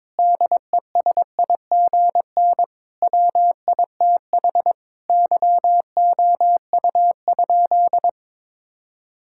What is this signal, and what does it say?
Morse code, heard as a keyed tone, about 22 wpm, 700 Hz. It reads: DESIGN WITH YOU?
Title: Sentences from Top 1000 Words - Review Part 5 - 22wpm